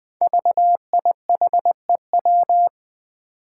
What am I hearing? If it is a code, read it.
Morse code, heard as a keyed tone, about 20 wpm, 700 Hz. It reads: VIHEW
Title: Code Groups - 5 Letter:Numbers - 20wpm